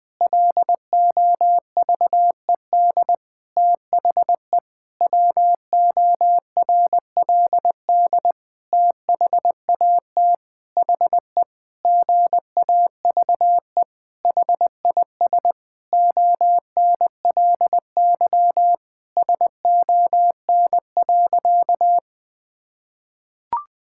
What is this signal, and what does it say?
Morse code, heard as a keyed tone, about 20 wpm, 700 Hz. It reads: LOVED THE WORLD THAT HE GAVE HIS ONLY SON.
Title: Sentences with 8-Word Phrases - 20wpm